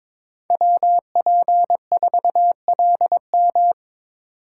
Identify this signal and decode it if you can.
Morse code, heard as a keyed tone, about 22 wpm, 700 Hz. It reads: WP4LM
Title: USA Call Signs - 5 Characters - 3x Character Spacing - 22wpm